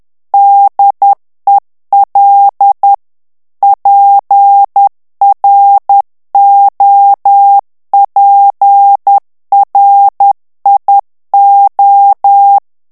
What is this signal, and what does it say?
Morse code, heard as a keyed tone, about 10 wpm, 800 Hz. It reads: DEL PROPRIO
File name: ITALIEN texte clair 2 vitesse 10 mots minute